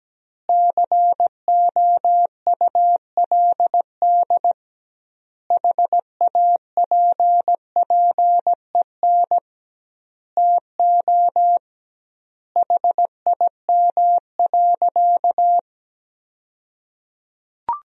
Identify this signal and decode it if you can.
Morse code, heard as a keyed tone, about 17 wpm, 700 Hz. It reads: COULD HAPPEN TO HIM.
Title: 2x Word-Spacing - Sentences from Top 400 Words - 17wpm